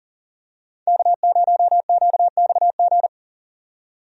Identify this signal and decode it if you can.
Morse code, heard as a keyed tone, about 40 wpm, 700 Hz. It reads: K0QXG
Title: USA Call Signs - 5 Characters - 40wpm